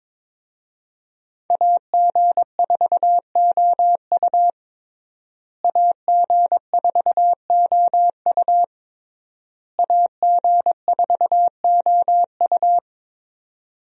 Morse code, heard as a keyed tone, about 22 wpm, 700 Hz. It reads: AG4OU AG4OU AG4OU
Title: USA Call Signs - 5 Characters - Repeated 3x - 22wpm